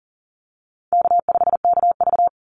Morse code, heard as a keyed tone, about 40 wpm, 700 Hz. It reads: K5KV
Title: Call Signs Repeated 3x - 40wpm